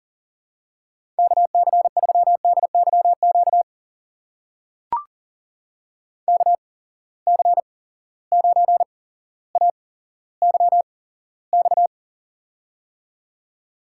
Morse code, heard as a keyed tone, about 41 wpm, spaced as for 17 wpm, 700 Hz. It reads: KC3DYQ KC9AYX
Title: USA Call Signs - 6 Characters - 8x Character Spacing - 40wpm